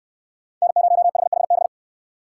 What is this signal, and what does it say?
Morse code, heard as a keyed tone, about 68 wpm, 700 Hz. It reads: N0HHB